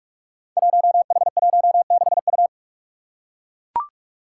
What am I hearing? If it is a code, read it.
Morse code, heard as a keyed tone, about 44 wpm, 700 Hz. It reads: 1H16U
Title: Five Challenging Letter-Number Combinations 45wpm